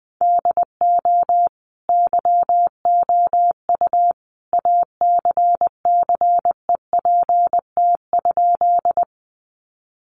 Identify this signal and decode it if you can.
Morse code, heard as a keyed tone, about 20 wpm, 700 Hz. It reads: DO YOU ACCEPT?
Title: Sentences from Top 1300 Words 20wpm